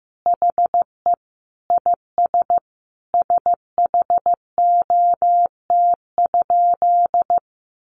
Morse code, heard as a keyed tone, about 15 wpm, 700 Hz. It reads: HE IS SHOT?
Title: Sentences from Top 1100 Words 15wpm